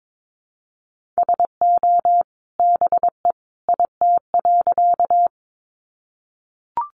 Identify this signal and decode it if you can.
Morse code, heard as a keyed tone, about 22 wpm, 700 Hz. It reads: SO BE IT.
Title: Sentences from Top 100 Words 22wpm